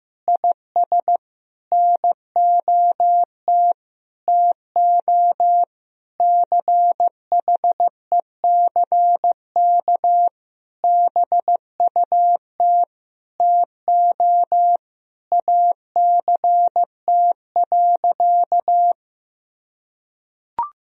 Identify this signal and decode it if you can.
Morse code, heard as a keyed tone, about 15 wpm, 700 Hz. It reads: IS NOT TO CHECK BUT TO ACT.
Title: Sentences from Top 1000 Words - Review Part 3 - 15wpm